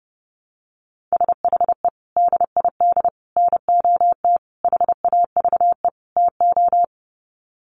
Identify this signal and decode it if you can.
Morse code, heard as a keyed tone, about 30 wpm, 700 Hz. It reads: SHE DID NOT HAVE TO
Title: Sets of 5 Words Encore2 30wpm